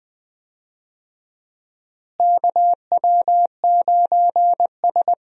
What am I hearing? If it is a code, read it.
Morse code, heard as a keyed tone, about 20 wpm, 700 Hz. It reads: KW9S
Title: USA Call Signs - 4 Characters - Repeated 2x - 20wpm